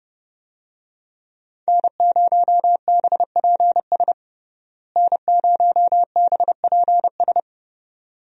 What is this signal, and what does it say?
Morse code, heard as a keyed tone, about 32 wpm, 700 Hz. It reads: N0BPS N0BPS
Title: USA Call Signs - 5 Characters - Repeated 2x - 30wpm